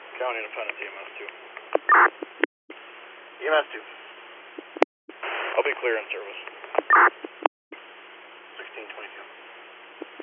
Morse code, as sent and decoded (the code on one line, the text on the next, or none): none